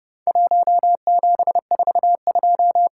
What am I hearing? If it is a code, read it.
Morse code, heard as a keyed tone, about 30 wpm, 700 Hz. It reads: 1742
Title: Worked All Europe DX Contest - QTC Exchange - 30wpm